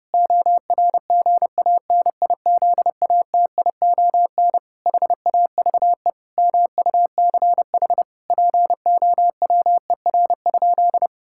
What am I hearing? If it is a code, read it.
Morse code, heard as a keyed tone, about 30 wpm, 700 Hz. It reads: ORGANIZATION HAVE MUCH POWER?
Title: Sentences from Top 1000 Words - Review Part 5 - 30wpm